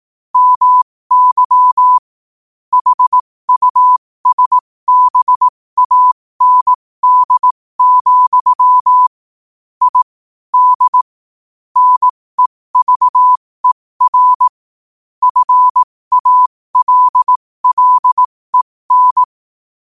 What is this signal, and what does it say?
Morse code, heard as a keyed tone, about 18 wpm, spaced as for 15 wpm, 1000 Hz. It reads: MY HUSBAND, I D NEVER FALLEN
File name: qotdfiles / 15wpm